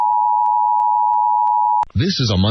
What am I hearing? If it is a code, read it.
Emergency Alert System (SAME) tones (the 853 and 960 Hz attention tone), with speech. It is heard from 0.0 to 1.9 s.